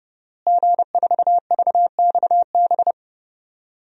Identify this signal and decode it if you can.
Morse code, heard as a keyed tone, about 30 wpm, 700 Hz. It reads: G4VXB
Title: DXCC-100 Call Signs - 5 Characters - Repeated 2x - 30wpm